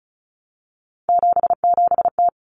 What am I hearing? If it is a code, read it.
Morse code, heard as a keyed tone, about 34 wpm, 700 Hz. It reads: 77T
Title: Common Cut Numbers - 3-Digit 35wpm